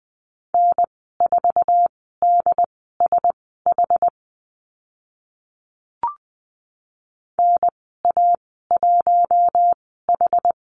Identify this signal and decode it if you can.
Morse code, heard as a keyed tone, about 20 wpm, spaced as for 14 wpm, 700 Hz. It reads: N4DSH NA1H
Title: 076 Lesson 13g - Letter L - Call Signs 20wpm